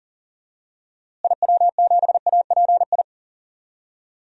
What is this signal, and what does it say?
Morse code, heard as a keyed tone, about 40 wpm, 700 Hz. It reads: IW7API
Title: DXCC-100 Call Signs - 6 Characters - Repeated 3x - 40wpm